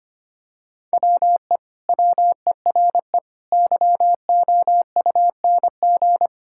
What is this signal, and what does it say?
Morse code, heard as a keyed tone, about 25 wpm, 700 Hz. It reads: WE WERE YOUNG
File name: practice / 25wpm / Sentences with 2-Word Phrases - 25wpm